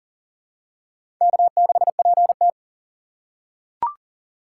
Morse code, heard as a keyed tone, about 40 wpm, 700 Hz. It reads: K6PT
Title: USA Call Signs - 4 Characters - 4x Character Spacing - 40wpm